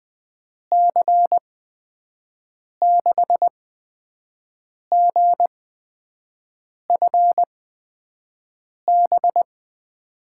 Morse code, heard as a keyed tone, about 20 wpm, spaced as for 5 wpm, 700 Hz. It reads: C6GFB
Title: USA Call Signs - 6 Characters - 8x Character Spacing - 20wpm